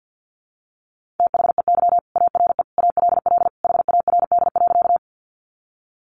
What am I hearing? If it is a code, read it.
Morse code, heard as a keyed tone, about 50 wpm, 700 Hz. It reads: THEY ARE ALL HARD.
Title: Additional Easy Phrases 50wpm